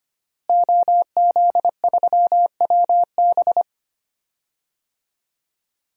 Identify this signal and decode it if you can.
Morse code, heard as a keyed tone, about 25 wpm, 700 Hz. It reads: OZ3WB
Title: DXCC-100 Call Signs - 5 Characters - Repeated 3x - 25wpm